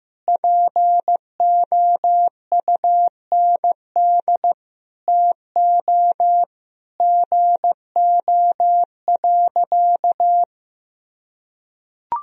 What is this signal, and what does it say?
Morse code, heard as a keyed tone, about 15 wpm, 700 Hz. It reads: POUND TO GO.
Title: Sentences from Top 1000 Words - Review Part 2 - 15wpm